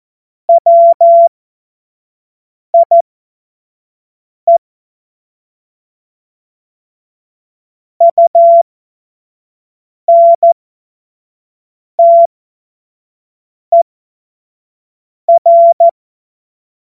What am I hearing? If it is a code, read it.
Morse code, heard as a keyed tone, about 14 wpm, spaced as for 5 wpm, 680 Hz. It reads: WIE UNTER